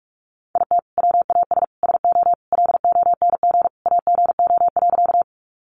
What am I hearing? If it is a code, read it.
Morse code, heard as a keyed tone, about 44 wpm, 700 Hz. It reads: IT WAS SO LONG AGO.
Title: Sentences from Top 500 Words 45wpm